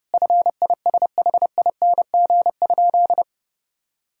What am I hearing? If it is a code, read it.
Morse code, heard as a keyed tone, about 30 wpm, 700 Hz. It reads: FISHING?
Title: Sentences from Top 1000 Words - Review Part 5 - 30wpm